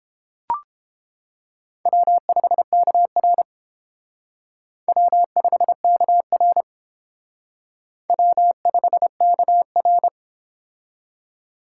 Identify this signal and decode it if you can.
Morse code, heard as a keyed tone, about 31 wpm, 690 Hz. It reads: W5KR W5KR W5KR